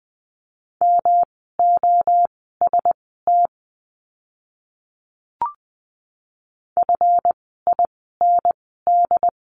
Morse code, heard as a keyed tone, about 20 wpm, spaced as for 15 wpm, 700 Hz. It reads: MOST FIND
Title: 139 Lesson 23f - Letter f - Words 20wpm